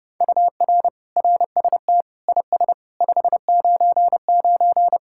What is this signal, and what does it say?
Morse code, heard as a keyed tone, about 30 wpm, 700 Hz. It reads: UR RST IS 599